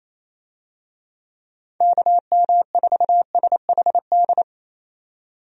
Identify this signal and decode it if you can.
Morse code, heard as a keyed tone, about 28 wpm, 700 Hz. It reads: KM4SHD
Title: USA Call Signs - 6 Characters - 28wpm